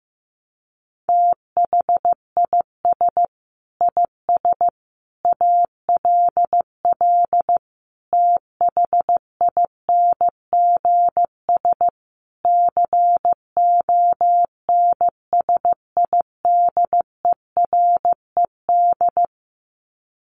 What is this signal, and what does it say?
Morse code, heard as a keyed tone, about 15 wpm, 700 Hz. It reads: THIS IS ALL THINGS CONSIDERED
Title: Sets of 5 Words 15wpm